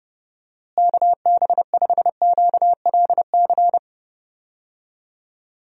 Morse code, heard as a keyed tone, about 30 wpm, 700 Hz. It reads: KB5QLC
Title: USA Call Signs - 6 Characters - Repeated 2x - 30wpm